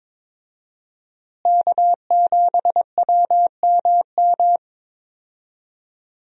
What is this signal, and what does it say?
Morse code, heard as a keyed tone, about 21 wpm, 700 Hz. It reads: K7WMM
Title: USA Call Signs - 5 Characters - Repeated 3x - 22wpm